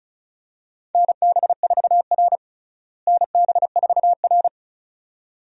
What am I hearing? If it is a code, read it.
Morse code, heard as a keyed tone, about 36 wpm, 700 Hz. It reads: NB4R NB4R
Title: USA Call Signs - 4 Characters - Repeated 2x - 35wpm